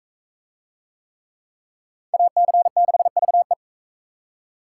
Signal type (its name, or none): Morse code